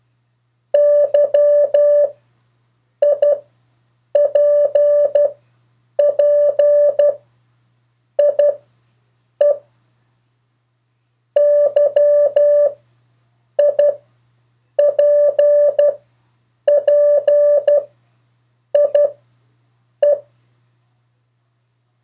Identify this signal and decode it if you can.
Morse code, heard as a keyed tone, about 12 wpm, spaced as for 7 wpm, 580 Hz. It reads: YIPPIE YIPPIE
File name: 2of4-words-1